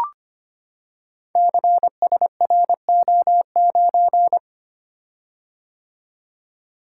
Morse code, heard as a keyed tone, about 25 wpm, 700 Hz. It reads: CSRO9
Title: Code Groups - 5 Letter:Numbers - 25wpm